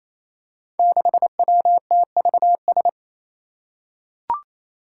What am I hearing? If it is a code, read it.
Morse code, heard as a keyed tone, about 27 wpm, 700 Hz. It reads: 6WTVS
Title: Five Challenging Letter-Number Combinations 28wpm